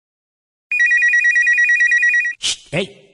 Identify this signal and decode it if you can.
Speech. There is no code.